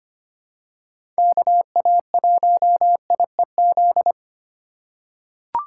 Morse code, heard as a keyed tone, about 25 wpm, 700 Hz. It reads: KA1IEZ
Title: USA Call Signs - 4x Character Spacing - 25wpm